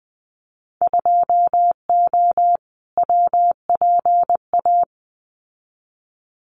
Morse code, heard as a keyed tone, about 20 wpm, 700 Hz. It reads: 2O WPA
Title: Winter Field Day Exchanges Encore 20wpm